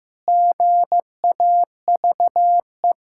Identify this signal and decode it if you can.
Morse code, heard as a keyed tone, about 16 wpm, 700 Hz. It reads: GAVE
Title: Pangram Challenge v2 - 15wpm